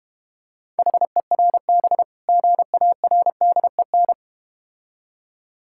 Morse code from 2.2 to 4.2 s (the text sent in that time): GARDEN